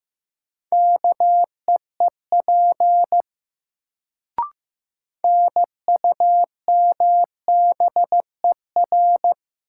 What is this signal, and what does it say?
Morse code, heard as a keyed tone, about 15 wpm, 700 Hz. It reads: KEEP NUMBER
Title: Sets of 2 Words - New Word Drill - 15wpm